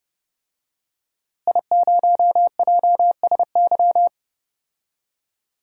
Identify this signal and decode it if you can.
Morse code, heard as a keyed tone, about 30 wpm, 700 Hz. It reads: I0JSY
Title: DXCC-100 Call Signs - 5 Characters - Repeated 3x - 30wpm